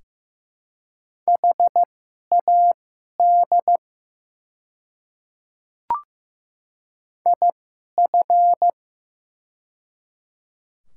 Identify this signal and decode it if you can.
Morse code, heard as a keyed tone, about 15 wpm, spaced as for 11 wpm, 700 Hz. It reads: HAD IF
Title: 2x Character Spacing - Top 300 Words - 15wpm